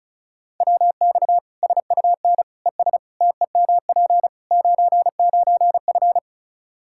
Morse code, heard as a keyed tone, about 36 wpm, 700 Hz. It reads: WX SUN ES TEMP 99F
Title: Practice QSO - Exchange 2 - Info - 35wpm